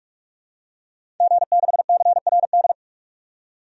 Morse code, heard as a keyed tone, about 43 wpm, 700 Hz. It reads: G6CRD
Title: DXCC-100 Call Signs - 5 Characters - 45wpm